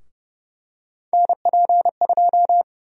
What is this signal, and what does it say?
Morse code, heard as a keyed tone, about 31 wpm, 700 Hz. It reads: NP2
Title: USA Call Sign Prefixes - Repeated 2x - 30wpm